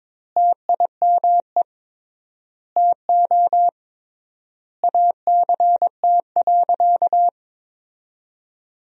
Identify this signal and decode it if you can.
Morse code, heard as a keyed tone, about 21 wpm, 700 Hz. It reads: TIME TO ACT.
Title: 3x Word-Spacing - Sentences from Top 200 Words - 22wpm